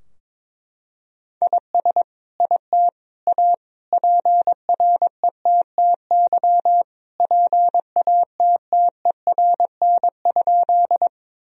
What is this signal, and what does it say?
Morse code, heard as a keyed tone, about 22 wpm, 700 Hz. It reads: IS IT A PRETTY PATTERN?
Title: Sentences from Top 1000 Words - Review Part 2 - 22wpm